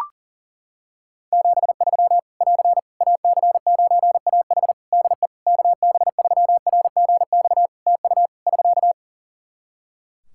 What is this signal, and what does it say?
Morse code, heard as a keyed tone, about 40 wpm, 700 Hz. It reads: 73 <AR> AC9AH DE KB3RGX TU <SK>
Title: Practice QSO - Exchange 4 - Ending - 40wpm